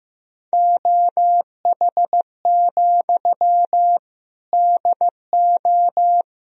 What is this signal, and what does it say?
Morse code, heard as a keyed tone, about 16 wpm, 700 Hz. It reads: OH, DO
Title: Sentences from Top 500 Words and Abbreviations 15wpm